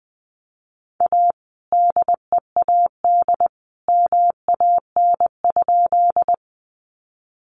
Morse code, heard as a keyed tone, about 20 wpm, 700 Hz. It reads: A DEAD MAN?
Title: Sentences from Top 1500 Words 20wpm